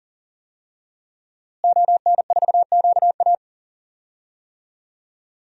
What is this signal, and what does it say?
Morse code, heard as a keyed tone, about 40 wpm, 700 Hz. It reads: ON4QA
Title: DXCC-100 Call Signs - 5 Characters - Repeated 2x - 40wpm